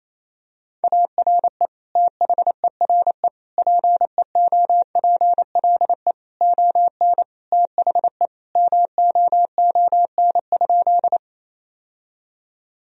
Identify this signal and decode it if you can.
Morse code, heard as a keyed tone, about 28 wpm, 700 Hz. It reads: ARE THERE PEOPLE ON THE MOON?
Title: Sentences from Top 1000 Words - Review Part 3 - 28wpm